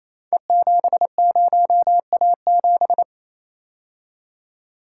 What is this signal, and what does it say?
Morse code, heard as a keyed tone, about 27 wpm, 700 Hz. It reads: E70A7
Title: Code Groups - 5 Letter:Numbers - 28wpm